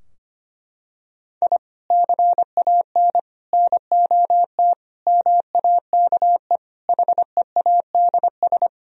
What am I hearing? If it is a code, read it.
Morse code, heard as a keyed tone, about 25 wpm, 700 Hz. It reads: I CAN NOT MAKE HEADS
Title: Sentences from Top 1000 Words - Review Part 3 - 25wpm